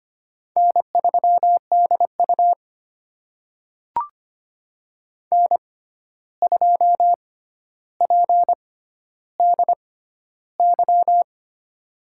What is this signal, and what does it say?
Morse code, heard as a keyed tone, about 25 wpm, spaced as for 12 wpm, 700 Hz. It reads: N3DU N2PDY
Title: USA Call Signs - 6x Character Spacing - 25wpm